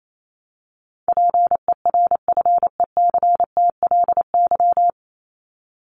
Morse code, heard as a keyed tone, about 28 wpm, 700 Hz. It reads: PERFECTLY
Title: LY Suffix - Top 100 Words with Base Words 28wpm